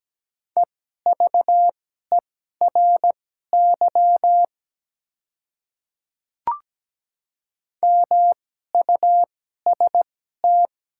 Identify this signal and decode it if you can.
Morse code, heard as a keyed tone, about 17 wpm, spaced as for 12 wpm, 700 Hz. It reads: EVERY MUST
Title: 2x Character Spacing - Top 300 Words - 17wpm